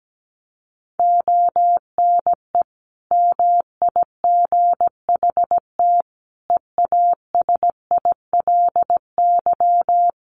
Morse code, heard as a keyed tone, about 17 wpm, 700 Hz. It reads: ONE MIGHT EASILY
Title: Sentences from Top 1400 Words 17wpm